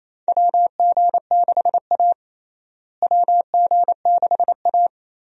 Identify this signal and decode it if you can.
Morse code, heard as a keyed tone, about 27 wpm, 700 Hz. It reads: WG6A WG6A
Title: USA Call Signs - 4 Characters - Repeated 2x - 28wpm